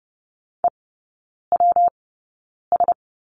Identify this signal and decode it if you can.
Morse code, heard as a keyed tone, about 30 wpm, spaced as for 9 wpm, 700 Hz. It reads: EWS
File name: Lesson 19e - Letter w - Three Characters 30wpm